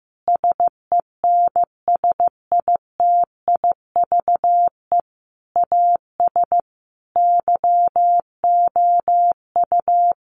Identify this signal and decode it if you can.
Morse code, heard as a keyed tone, about 15 wpm, 700 Hz. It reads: SENSITIVE AS YOU
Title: Sentences from Top 1400 Words 15wpm